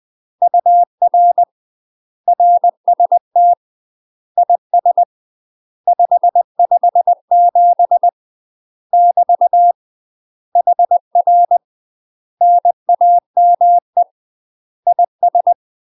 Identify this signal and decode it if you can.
Morse code, heard as a keyed tone, about 20 wpm, 700 Hz. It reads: UR RST IS 557 <BT> HR NAME IS